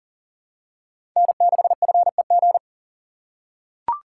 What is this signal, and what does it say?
Morse code, heard as a keyed tone, about 40 wpm, 700 Hz. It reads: N6FEG
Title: USA Call Signs - 5 Characters - 2x Character Spacing - 40wpm